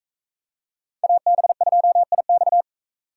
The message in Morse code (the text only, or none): AB2IX